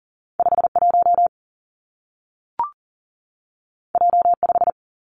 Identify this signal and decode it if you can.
Morse code, heard as a keyed tone, about 39 wpm, 700 Hz. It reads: 51 J5
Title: Two Challenging Letter-Number Combinations 40wpm